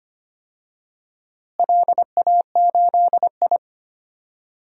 Morse code, heard as a keyed tone, about 25 wpm, 700 Hz. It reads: LA8I
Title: DXCC-100 Call Signs - 4 Characters - Repeated 3x - 25wpm